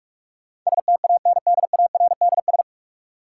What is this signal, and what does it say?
Morse code, heard as a keyed tone, about 45 wpm, 700 Hz. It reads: STANDARDS
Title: Sentences from Top 1000 Words - Review Part 6 - New Word Drill - 45wpm